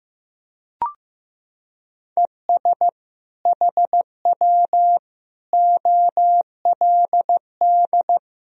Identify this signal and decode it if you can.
Morse code, heard as a keyed tone, about 15 wpm, 700 Hz. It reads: ES HW OLD
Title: Sentences from Top 200 Words and Abbreviations 15wpm